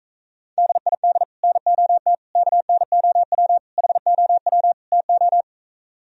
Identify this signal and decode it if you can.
Morse code, heard as a keyed tone, about 45 wpm, 700 Hz. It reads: DID NOT KNOW HOW TO